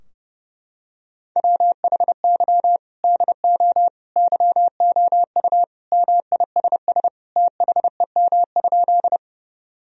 Morse code, heard as a keyed tone, about 28 wpm, 700 Hz. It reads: WHY DO YOU MISS THEM?
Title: Sentences from Top 1000 Words - Review Part 3 - 30wpm